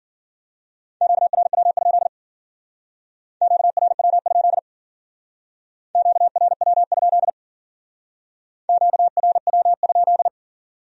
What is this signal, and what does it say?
Morse code, heard as a keyed tone, about 49 wpm, 700 Hz. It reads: QRW? QRW? QRW? QRW?
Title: Speed-Racing Q-Codes - 40wpm